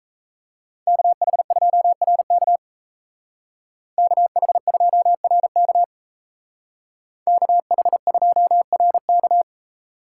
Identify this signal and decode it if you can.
Morse code, heard as a keyed tone, about 33 wpm, 700 Hz. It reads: KH2RK KH2RK KH2RK
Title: Speed Racing - USA 2x2 Call Signs 28wpm